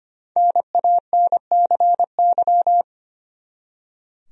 Morse code, heard as a keyed tone, about 25 wpm, 700 Hz. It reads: NANCY
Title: USA Common Names - 25wpm